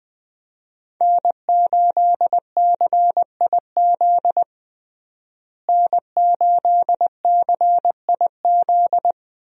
Morse code, heard as a keyed tone, about 21 wpm, 700 Hz. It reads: N8CIZ N8CIZ
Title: USA Call Signs - 5 Characters - Repeated 2x - 20wpm